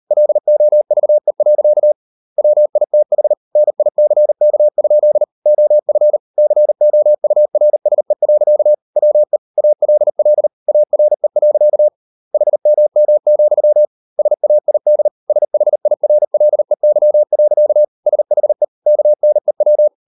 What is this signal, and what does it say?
Morse code, heard as a keyed tone, about 40 wpm, 600 Hz. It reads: LOVE. WITH NICK? OF COURSE. WE ALL ARE. HMM, SAID SHIRLEY. SHE KNEW